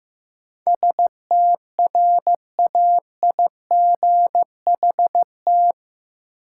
Morse code, heard as a keyed tone, about 15 wpm, 700 Hz. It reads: STRAIGHT
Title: Sentences from Top 1000 Words - New Word Drill - 15wpm